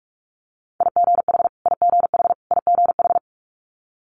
Morse code, heard as a keyed tone, about 46 wpm, 700 Hz. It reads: IGH IGH IGH
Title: Multi-letter Phonemes Repeated 3x 45wpm